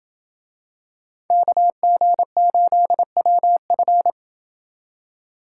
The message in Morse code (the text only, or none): KG8WF